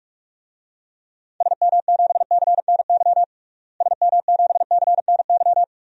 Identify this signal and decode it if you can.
Morse code, heard as a keyed tone, about 41 wpm, 700 Hz. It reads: SM7CNY SM7CNY
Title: DXCC-100 Call Signs - 6 Characters - Repeated 3x - 45wpm